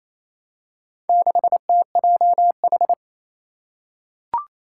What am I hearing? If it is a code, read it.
Morse code, heard as a keyed tone, about 28 wpm, 700 Hz. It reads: 6TJH
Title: Four Challenging Letter-Number Combinations 28wpm